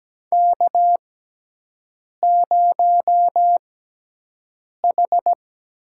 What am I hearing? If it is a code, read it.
Morse code, heard as a keyed tone, about 17 wpm, spaced as for 6 wpm, 700 Hz. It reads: K0H
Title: USA Call Signs - 4 Characters - 6x Character Spacing - 17wpm